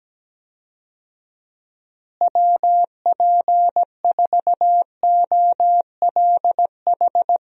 Morse code, heard as a keyed tone, about 17 wpm, 700 Hz. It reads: WP4OLH
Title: USA Call Signs - 6 Characters - Repeated 3x - 17wpm